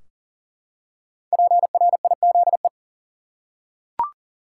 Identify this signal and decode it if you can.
Morse code, heard as a keyed tone, about 37 wpm, 700 Hz. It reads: PRIZE
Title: Five Letter Words - 40wpm